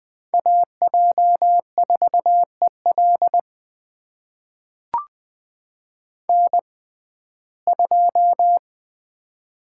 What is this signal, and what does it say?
Morse code, heard as a keyed tone, about 20 wpm, spaced as for 14 wpm, 700 Hz. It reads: AJ4EL N2
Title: USA Call Signs - 6x Character Spacing - 20wpm